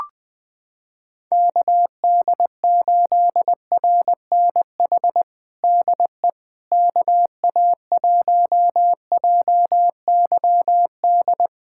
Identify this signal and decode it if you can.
Morse code, heard as a keyed tone, about 21 wpm, 700 Hz. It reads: KD8RNH DE KA1JYD
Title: USA Callsign DE Callsign 20wpm